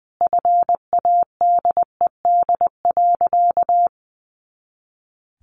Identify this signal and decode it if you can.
Morse code, heard as a keyed tone, about 20 wpm, 700 Hz. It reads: FADED.
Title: A Sound Of Thunder 20wpm